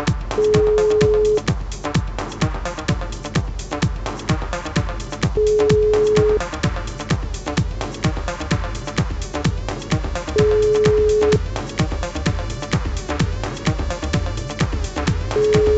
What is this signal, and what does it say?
Ringback tone (Europe, 425 Hz).